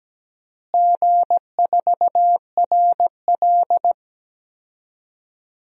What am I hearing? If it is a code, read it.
Morse code, heard as a keyed tone, about 17 wpm, 700 Hz. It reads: G4RL
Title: DXCC-100 Call Signs - 4 Characters - Repeated 3x - 17wpm